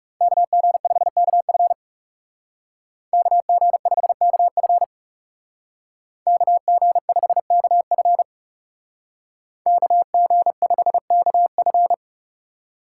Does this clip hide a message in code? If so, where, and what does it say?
Yes: Morse code from 0.2 to 11.9 s, heard as a keyed tone, about 38 wpm, 700 Hz. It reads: KG5KF KG5KF KG5KF KG5KF